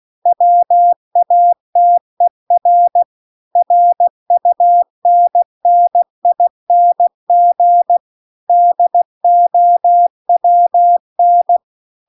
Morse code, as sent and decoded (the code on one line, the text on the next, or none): .-- .- - . .-. / .-. ..- -. -. .. -. --. / -.. --- .-- -.
WATER RUNNING DOWN